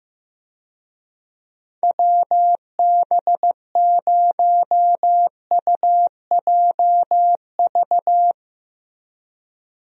Morse code, heard as a keyed tone, about 15 wpm, 700 Hz. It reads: WB0UJV
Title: USA Call Signs - 6 Characters - Repeated 2x - 15wpm